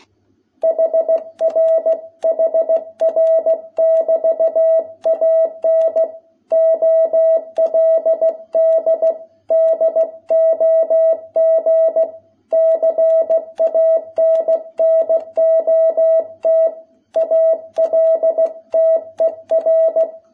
Morse code, heard as a keyed tone, about 16 wpm, 660 Hz. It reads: HR HR<BT>AN OLD DOG CANNOT ALTER
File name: jh8cbh / 2021cw